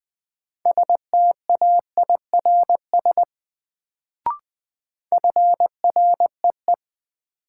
Morse code, heard as a keyed tone, about 20 wpm, 700 Hz. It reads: STAIRS FREE
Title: Sets of 5 Words Encore - New Word Drill - 20wpm